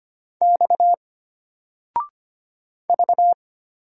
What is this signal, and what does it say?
Morse code, heard as a keyed tone, about 25 wpm, 700 Hz. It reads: X V
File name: Single-Letter-Number-25wpm